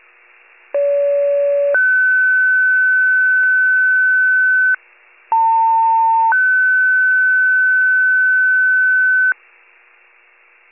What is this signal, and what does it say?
Two-tone page, heard as a one-second tone, then a three-second tone, about 570 then 1500 Hz, about 900 then 1500 Hz. It is heard from 0.7 to 9.3 s.
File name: pager / 78 FIRE / wcalert